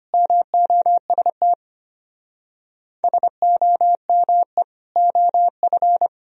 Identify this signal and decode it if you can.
Morse code, heard as a keyed tone, about 31 wpm, 700 Hz. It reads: MOST SOME OF
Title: Speed-Racing - Sets of 4 Words 25wpm